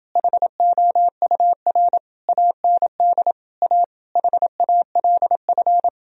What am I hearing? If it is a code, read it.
Morse code, heard as a keyed tone, about 29 wpm, 700 Hz. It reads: HOUR AND A HALF